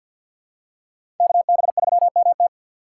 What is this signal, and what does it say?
Morse code, heard as a keyed tone, about 51 wpm, 690 Hz. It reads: KB3MT
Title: USA Call Signs - 5 Characters - 4x Character Spacing - 50wpm